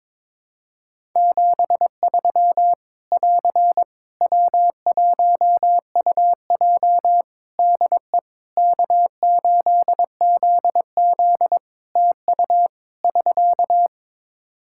Morse code, heard as a keyed tone, about 21 wpm, 700 Hz. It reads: 73 <AR> W1UJ DE K8ZZ TU <SK>
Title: Practice QSO - Exchange 4 - Ending - 22wpm